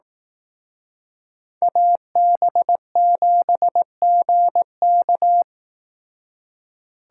Morse code, heard as a keyed tone, about 18 wpm, 700 Hz. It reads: AB7GK
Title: Speed Racing - USA 2x2 Call Signs 15wpm